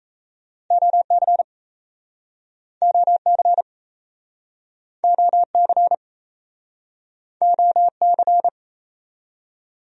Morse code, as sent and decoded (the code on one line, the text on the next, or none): --- -.-. / --- -.-. / --- -.-. / --- -.-.
OC OC OC OC